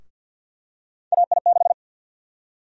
Morse code, heard as a keyed tone, about 46 wpm, 700 Hz. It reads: AI6
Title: USA Call Sign Prefixes - Repeated 2x - 50wpm